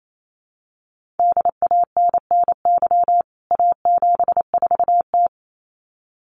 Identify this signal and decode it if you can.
Morse code, heard as a keyed tone, about 28 wpm, 700 Hz. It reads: DANNY A74T
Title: CWT Exchange 28wpm